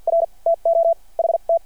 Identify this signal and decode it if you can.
Morse code, heard as a keyed tone, about 52 wpm, spaced as for 40 wpm, 660 Hz. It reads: R TO HT
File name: archived / CRSnet